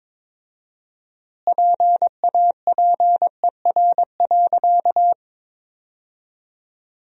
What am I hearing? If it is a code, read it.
Morse code, heard as a keyed tone, about 22 wpm, 700 Hz. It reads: PAPER.
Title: 4x Word-Spacing - Sentences from Top 300 Words - 22wpm